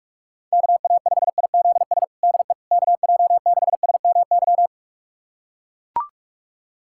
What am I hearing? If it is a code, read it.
Morse code, heard as a keyed tone, about 41 wpm, 700 Hz. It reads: KA5IZS DE KJ6SMY
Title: USA Callsign DE Callsign - 6 Characters - 45wpm